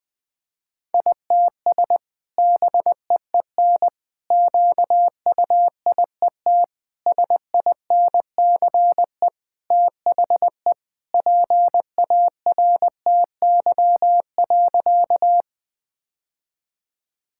Morse code, heard as a keyed tone, about 20 wpm, 700 Hz. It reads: ITS BEEN QUIET SINCE THE PARTY.
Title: Sentences with 2-Word Phrases - 20wpm